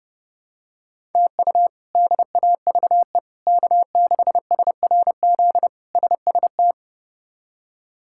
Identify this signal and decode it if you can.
Morse code, heard as a keyed tone, about 30 wpm, 700 Hz. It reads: TU DAVE K6SRZ SST
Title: K1USN SST QSO Practice - 30wpm